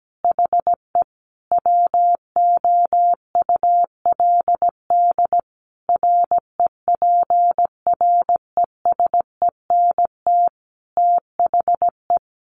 Morse code, heard as a keyed tone, about 17 wpm, 700 Hz. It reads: HE WOULD REPRESENT THE
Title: Sentences from Top 1200 Words 17wpm